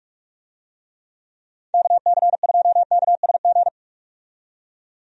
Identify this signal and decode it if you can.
Morse code, heard as a keyed tone, about 45 wpm, 700 Hz. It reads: KC2KSG
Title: USA Call Signs - 6 Characters - 45wpm